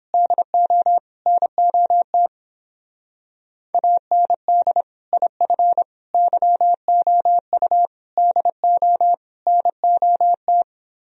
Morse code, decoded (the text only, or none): DO NOT AND IF YOU DO NOT